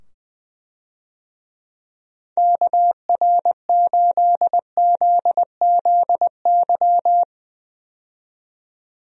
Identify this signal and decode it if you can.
Morse code, heard as a keyed tone, about 20 wpm, 700 Hz. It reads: KR8ZZY
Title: USA Call Signs - 6 Characters - 20wpm